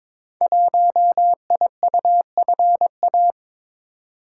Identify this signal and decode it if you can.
Morse code, heard as a keyed tone, about 22 wpm, 700 Hz. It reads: 1IUFA
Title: Code Groups - 5 Letter:Numbers - 22wpm